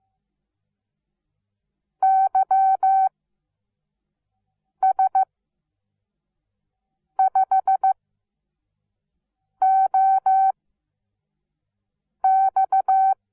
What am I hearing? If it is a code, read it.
Morse code, heard as a keyed tone, about 15 wpm, spaced as for 4 wpm, 770 Hz. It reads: YS5OX
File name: Onlinelehrgang / morsen / DJ4UF Morsen17